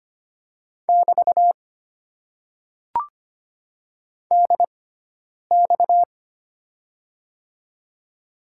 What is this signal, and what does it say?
Morse code, heard as a keyed tone, about 25 wpm, spaced as for 9 wpm, 700 Hz. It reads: <BT> DX
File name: Top 90 QSO Elements - 6x Character Spacing - 25wpm